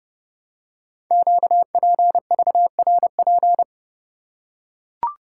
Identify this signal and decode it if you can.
Morse code, heard as a keyed tone, about 30 wpm, 690 Hz. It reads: QPVRP
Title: Code Groups - 5 Letters - 30wpm